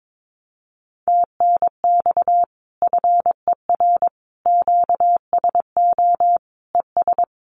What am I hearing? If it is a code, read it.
Morse code, heard as a keyed tone, about 22 wpm, 700 Hz. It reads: TNX FER QSO ES
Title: Common QSO Phrases Encore 22wpm